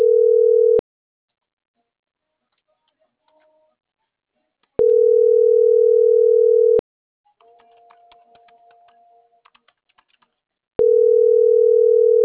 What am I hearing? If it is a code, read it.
Ringback tone (North America), 440 + 480 Hz.